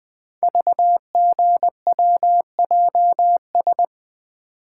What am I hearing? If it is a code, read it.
Morse code, heard as a keyed tone, about 20 wpm, 700 Hz. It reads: VGWJS
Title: Code Groups - 5 Letters - 20wpm